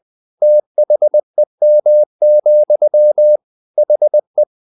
Morse code, heard as a keyed tone, about 20 wpm, 600 Hz. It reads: THEM, HE
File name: the old man and the sea part 2 20wpm